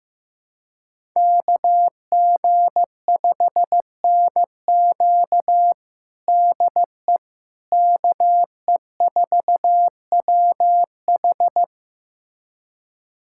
Morse code, heard as a keyed tone, about 15 wpm, 700 Hz. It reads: KG5NQ DE KE4WH
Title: USA Callsign DE Callsign 15wpm